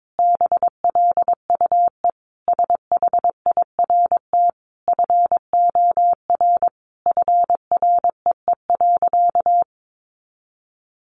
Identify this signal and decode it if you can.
Morse code, heard as a keyed tone, about 22 wpm, 700 Hz. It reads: BLUE SHIRT FOR FREE.
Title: Additional Easy Phrases 22wpm